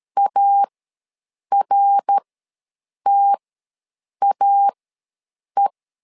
Morse code, heard as a keyed tone, about 13 wpm, spaced as for 7 wpm, 800 Hz. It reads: ARTAE
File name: LEZIONI / 3 MWRO / 3 MWRO ws 12wpm 1914